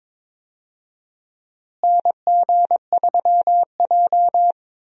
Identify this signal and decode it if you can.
Morse code, heard as a keyed tone, about 22 wpm, 700 Hz. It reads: NG3J
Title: USA Call Signs - 4 Characters - Repeated 2x - 22wpm